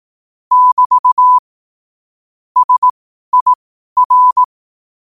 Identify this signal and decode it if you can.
Morse code, heard as a keyed tone, about 18 wpm, spaced as for 12 wpm, 1000 Hz. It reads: <BT> SIR
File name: qotdfiles / 12wpm